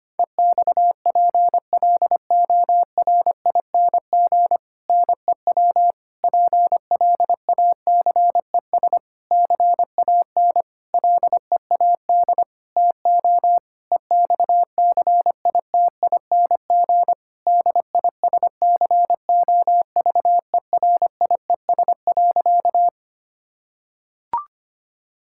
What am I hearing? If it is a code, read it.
Morse code, heard as a keyed tone, about 25 wpm, 700 Hz. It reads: EXPLORING NEW PLACES CAN LEAD TO EXCITING DISCOVERIES.
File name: practice / 25wpm / Difficult Sentences with Hints - 25wpm